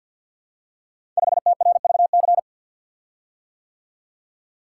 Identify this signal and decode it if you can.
Morse code, heard as a keyed tone, about 50 wpm, 700 Hz. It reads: 5TRVC
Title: Code Groups - 5 Letter:Numbers - 50wpm